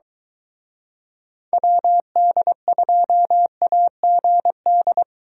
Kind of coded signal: Morse code

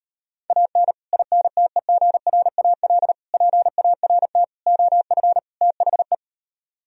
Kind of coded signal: Morse code

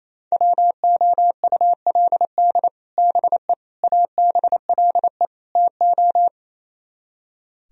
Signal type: Morse code

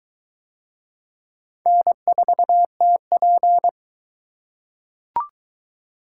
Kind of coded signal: Morse code

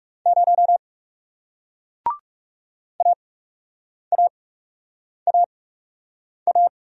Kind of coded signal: Morse code

